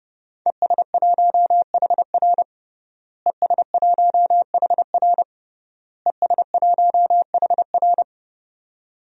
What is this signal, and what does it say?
Morse code, heard as a keyed tone, about 30 wpm, 700 Hz. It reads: ES1HR ES1HR ES1HR
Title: DXCC-100 Call Signs - 5 Characters - Repeated 3x - 30wpm